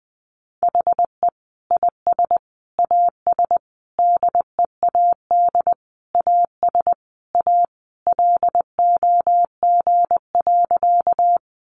Morse code, heard as a keyed tone, about 20 wpm, 700 Hz. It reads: HE IS AS DEAD AS A LOG.
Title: Sentences from Top 1500 Words 20wpm